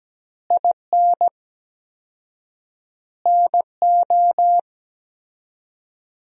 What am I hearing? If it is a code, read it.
Morse code, heard as a keyed tone, about 17 wpm, 700 Hz. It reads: IN NO
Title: 4x Word-Spacing - Sentences from Top 400 Words - 17wpm